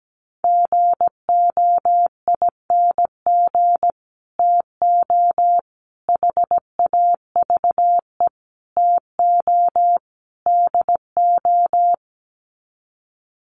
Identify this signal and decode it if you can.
Morse code, heard as a keyed tone, about 17 wpm, 700 Hz. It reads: GOING TO HAVE TO DO
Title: Sets of 5 Words Encore 17wpm